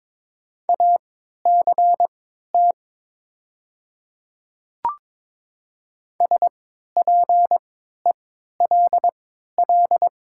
Morse code, heard as a keyed tone, about 22 wpm, spaced as for 12 wpm, 700 Hz. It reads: ACT SPELL
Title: 3x Character Spacing - Top 500 Words - 22wpm